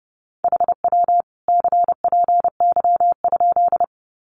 Morse code, heard as a keyed tone, about 30 wpm, 700 Hz. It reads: HW CPY?
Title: Common QSO Phrases 30wpm